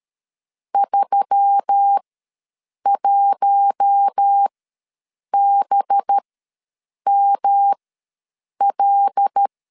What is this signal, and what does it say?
Morse code, heard as a keyed tone, about 13 wpm, spaced as for 7 wpm, 800 Hz. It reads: 31BML
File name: LEZIONI / 10 SIGNS / 10D COMA ws 12wpm 3618